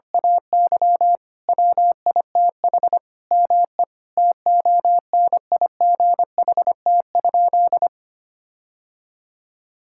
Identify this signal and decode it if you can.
Morse code, heard as a keyed tone, about 25 wpm, 700 Hz. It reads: AY WITH ME TONIGHT?
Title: Sentences from Top 1600 Words 25wpm